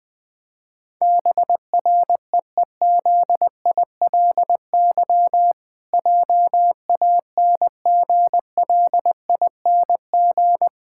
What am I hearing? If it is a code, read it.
Morse code, heard as a keyed tone, about 20 wpm, 700 Hz. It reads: BREEZILY JANGLING
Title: Pangram Challenge v2 - 20wpm